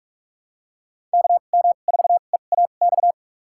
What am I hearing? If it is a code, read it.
Morse code, heard as a keyed tone, about 42 wpm, spaced as for 31 wpm, 700 Hz. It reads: KM4EAX
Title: USA Call Signs - 2x Character Spacing - 45wpm